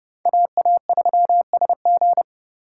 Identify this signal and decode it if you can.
Morse code, heard as a keyed tone, about 37 wpm, 700 Hz. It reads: AA3SG
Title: Speed Racing - USA 2x2 Call Signs 30wpm